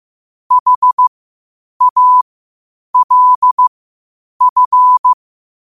Morse code, heard as a keyed tone, about 15 wpm, spaced as for 8 wpm, 1000 Hz. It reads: HALF
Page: Quotes for Thu, 14 Aug 2025 in Morse Code at 8 words per minute.